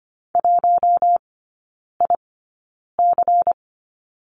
Morse code, heard as a keyed tone, about 25 wpm, spaced as for 9 wpm, 700 Hz. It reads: 1IC
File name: course / 25:10wpm / 093 Lesson 16e - Letter c - Three Characters 25wpm